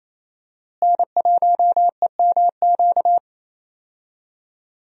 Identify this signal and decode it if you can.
Morse code, heard as a keyed tone, about 28 wpm, 700 Hz. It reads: N1EMQ
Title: USA Call Signs - 5 Characters - Repeated 2x - 28wpm